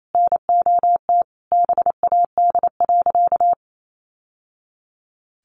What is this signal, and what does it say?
Morse code, heard as a keyed tone, about 28 wpm, 700 Hz. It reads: NOT BAD.
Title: Sentences from Top 700 Words 28wpm